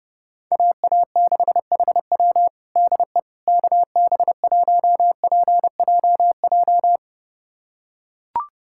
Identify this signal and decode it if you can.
Morse code, heard as a keyed tone, about 30 wpm, 700 Hz. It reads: AA6HW DE KB1PJJ
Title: USA Callsign DE Callsign 30wpm